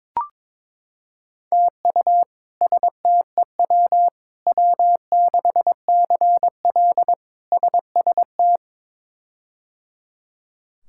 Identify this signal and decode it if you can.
Morse code, heard as a keyed tone, about 22 wpm, 700 Hz. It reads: TU STEW W6CL SST
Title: K1USN SST QSO Practice - 22wpm